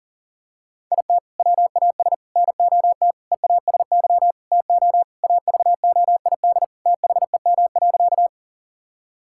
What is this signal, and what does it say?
Morse code, heard as a keyed tone, about 40 wpm, 700 Hz. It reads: IT WAS NOT EASY TO AVOID THEM.
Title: Sentences from Top 1000 Words - Review Part 6 - 40wpm